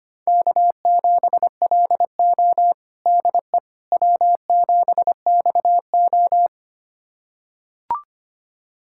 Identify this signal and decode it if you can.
Morse code, heard as a keyed tone, about 25 wpm, 700 Hz. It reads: K7LO DE W7XO
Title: USA Callsign DE Callsign - 4 Characters - 25wpm